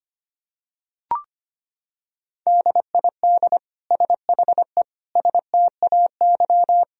Morse code, heard as a keyed tone, about 25 wpm, 700 Hz. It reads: DID SHE STAY
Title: Sentences from Top 500 Words and Abbreviations 25wpm